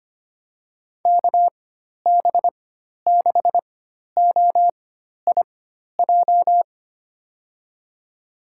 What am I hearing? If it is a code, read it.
Morse code, heard as a keyed tone, about 25 wpm, spaced as for 12 wpm, 700 Hz. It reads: KB6OIJ
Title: USA Call Signs - 6 Characters - 4x Character Spacing - 25wpm